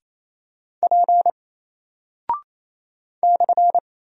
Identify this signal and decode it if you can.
Morse code, heard as a keyed tone, about 28 wpm, 700 Hz. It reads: P /
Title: Single Character (Letter:Number:Punctuation:Prosign) Alt 28wpm